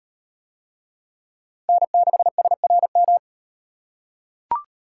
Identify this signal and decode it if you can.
Morse code, heard as a keyed tone, about 38 wpm, 700 Hz. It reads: N6SRM